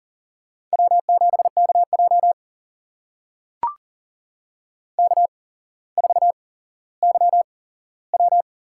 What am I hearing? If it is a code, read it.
Morse code, heard as a keyed tone, about 44 wpm, spaced as for 17 wpm, 700 Hz. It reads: W7KJ K4YW
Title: USA Call Signs - 4 Characters - 8x Character Spacing - 40wpm